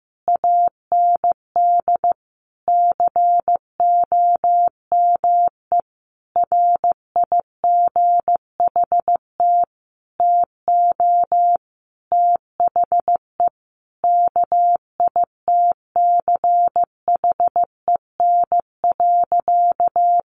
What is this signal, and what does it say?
Morse code, heard as a keyed tone, about 15 wpm, 700 Hz. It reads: AND COME RIGHT TO THE KITCHEN.
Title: Sentences from Top 1400 Words 15wpm